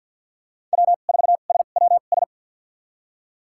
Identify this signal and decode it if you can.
Morse code, heard as a keyed tone, about 52 wpm, spaced as for 36 wpm, 700 Hz. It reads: W4SWS
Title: USA Call Signs - 5 Characters - 2x Character Spacing - 50wpm